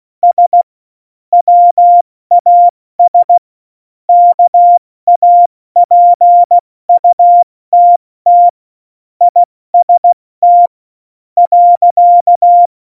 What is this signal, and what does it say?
Morse code, heard as a keyed tone, about 16 wpm, spaced as for 13 wpm, 700 Hz. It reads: S WAS KAPUTT IST.
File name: Morsekurs / Witze / Witz 29. Januar 2019, 70 F80